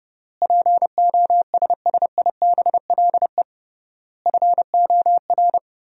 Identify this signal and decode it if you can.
Morse code, heard as a keyed tone, about 28 wpm, 700 Hz. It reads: POSSIBLE FOR
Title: 3x Word-Spacing - Sentences from Top 500 Words - 30wpm